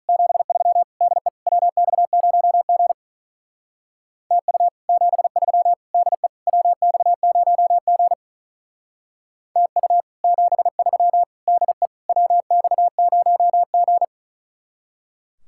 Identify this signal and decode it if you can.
Morse code, heard as a keyed tone, about 46 wpm, 700 Hz. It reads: 73 DE WX0G TU 73 DE WX0G TU 73 DE WX0G